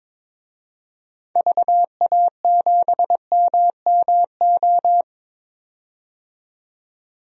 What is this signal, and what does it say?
Morse code, heard as a keyed tone, about 22 wpm, 700 Hz. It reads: VA7MMO
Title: DXCC-100 Call Signs - 6 Characters - 22wpm